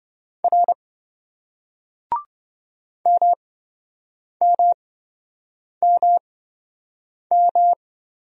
Morse code, heard as a keyed tone, about 26 wpm, spaced as for 7 wpm, 700 Hz. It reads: R MMMM